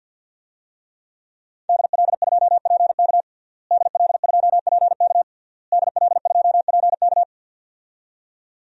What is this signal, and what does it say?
Morse code, heard as a keyed tone, about 52 wpm, 700 Hz. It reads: DL2PK DL2PK DL2PK
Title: DXCC-100 Call Signs - 5 Characters - Repeated 3x - 50wpm